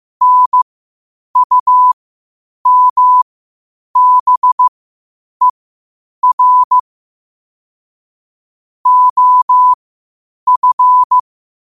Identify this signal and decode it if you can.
Morse code, heard as a keyed tone, about 15 wpm, spaced as for 8 wpm, 1000 Hz. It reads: NUMBER OF
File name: qotdfiles / 8wpm